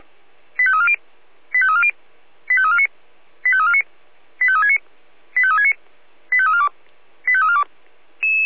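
Five-tone selective call (ZVEI), heard as quick runs of five tones, tones of about 70 ms. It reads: 85340 85340 85349 85349 85369 85369 85342 85342
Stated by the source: ZVEI five-tone selective call received by SDR.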